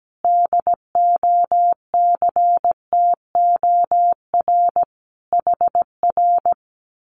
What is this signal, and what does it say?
Morse code, heard as a keyed tone, about 17 wpm, 700 Hz. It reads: DOCTOR HR
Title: Common QSO Phrases Encore 17wpm